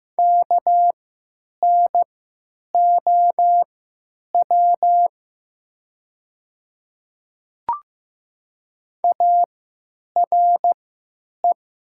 Morse code, heard as a keyed tone, about 15 wpm, spaced as for 9 wpm, 700 Hz. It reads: KNOW ARE
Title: 3x Character Spacing - Top 200 Words - 15wpm